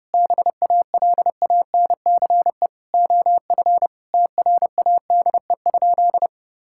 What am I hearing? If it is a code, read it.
Morse code, heard as a keyed tone, about 30 wpm, 700 Hz. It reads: BALANCE OF TRADE?
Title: Sentences from Top 1000 Words - Review Part 6 - 30wpm